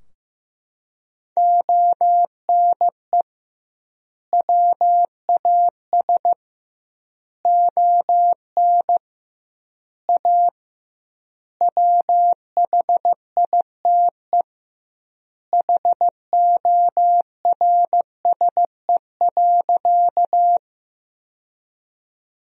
Morse code, heard as a keyed tone, about 15 wpm, 700 Hz. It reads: ONE WAS ON A WHITE HORSE.
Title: 2x Word-Spacing - Sentences from Top 300 Words - 15wpm